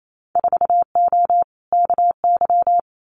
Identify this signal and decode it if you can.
Morse code, heard as a keyed tone, about 28 wpm, 700 Hz. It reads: 4O KY
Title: Winter Field Day Exchanges Encore 28wpm